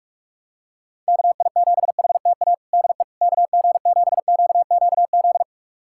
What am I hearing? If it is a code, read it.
Morse code, heard as a keyed tone, about 46 wpm, 700 Hz. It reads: KI7HTA DE KG7QQZ
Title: USA Callsign DE Callsign - 6 Characters - 45wpm